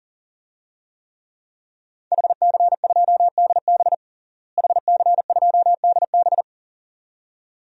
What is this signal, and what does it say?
Morse code, heard as a keyed tone, about 40 wpm, 700 Hz. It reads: HC2DB HC2DB
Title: DXCC-100 Call Signs - 5 Characters - Repeated 2x - 40wpm